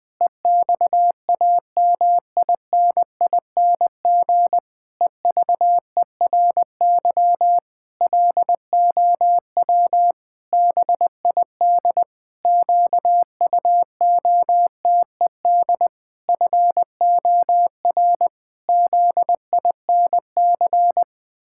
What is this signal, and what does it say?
Morse code, heard as a keyed tone, about 20 wpm, 700 Hz. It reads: EXAMINING EVERY LOW BID QUOTED FOR ZINC
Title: Pangram Challenge v2 - 20wpm